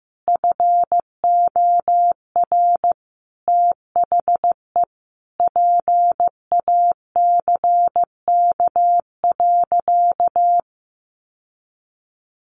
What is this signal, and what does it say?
Morse code, heard as a keyed tone, about 15 wpm, 700 Hz. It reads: FOR THE PACK.
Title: Sentences from Top 1300 Words 15wpm